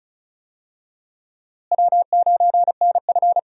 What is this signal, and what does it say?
Morse code, heard as a keyed tone, about 36 wpm, 700 Hz. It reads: W9NF
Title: USA Call Signs - 4 Characters - Repeated 2x - 35wpm